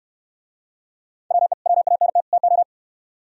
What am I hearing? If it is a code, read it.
Morse code, heard as a keyed tone, about 67 wpm, 700 Hz. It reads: WE WANT TO